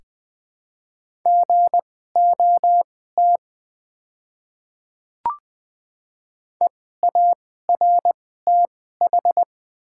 Morse code, heard as a keyed tone, about 20 wpm, spaced as for 14 wpm, 700 Hz. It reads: GOT EARTH
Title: 2x Character Spacing - Top 300 Words - 20wpm